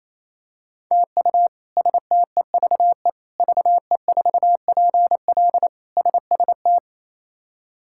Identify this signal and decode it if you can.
Morse code, heard as a keyed tone, about 28 wpm, 700 Hz. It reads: TU STEVE VE4PL SST
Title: K1USN SST QSO Practice - 28wpm